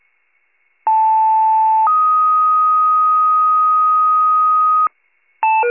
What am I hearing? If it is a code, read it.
Two-tone page, heard as a one-second tone, then a three-second tone, about 880 then 1250 Hz. It is heard from 0.9 to 4.9 s.
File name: 76 FIRE